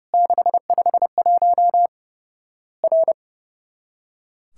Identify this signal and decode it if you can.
Morse code, heard as a keyed tone, about 30 wpm, 700 Hz. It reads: 651 R
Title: Worked All Europe DX Contest - QTC Exchange - 30wpm